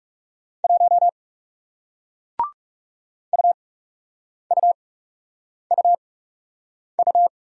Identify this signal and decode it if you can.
Morse code, heard as a keyed tone, about 39 wpm, spaced as for 8 wpm, 700 Hz. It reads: 1 UUUU